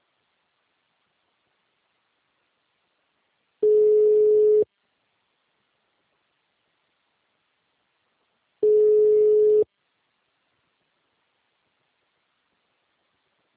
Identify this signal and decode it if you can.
Ringback tone (Europe, 425 Hz).